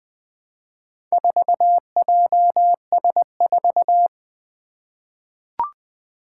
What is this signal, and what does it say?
Morse code, heard as a keyed tone, about 20 wpm, 700 Hz. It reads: 4JS4
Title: Four Challenging Letter-Number Combinations 20wpm